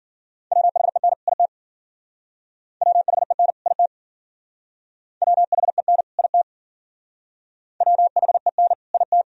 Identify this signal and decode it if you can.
Morse code, heard as a keyed tone, about 49 wpm, 700 Hz. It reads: WHEN IT WHEN IT WHEN IT WHEN IT
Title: Speed-Racing - Sets of 2 Words 40wpm